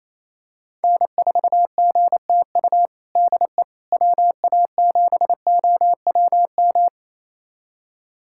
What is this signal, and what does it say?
Morse code, heard as a keyed tone, about 28 wpm, 700 Hz. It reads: N4GTU DE WA7OWM
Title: USA Callsign DE Callsign 28wpm